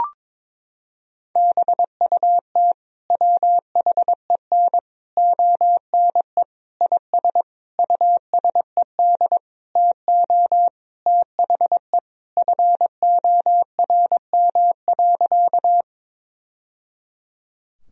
Morse code, heard as a keyed tone, about 22 wpm, 700 Hz. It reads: BUT WHEN ONE IS USED TO THE FORM.
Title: Sentences from Top 1000 Words - Review Part 1 - 22wpm